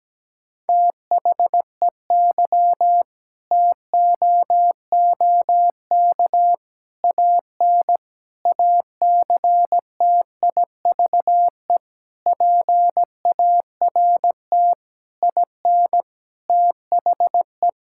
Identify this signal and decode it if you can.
Morse code, heard as a keyed tone, about 17 wpm, 700 Hz. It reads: THEY TOOK AN ACTIVE PART IN THE
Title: Sentences from Top 1000 Words - Review Part 6 - 17wpm